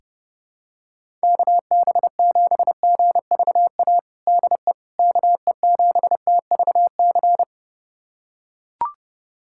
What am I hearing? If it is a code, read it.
Morse code, heard as a keyed tone, about 30 wpm, 700 Hz. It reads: KB7GVA DE KE7TVC
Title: USA Callsign DE Callsign - 6 Characters - 30wpm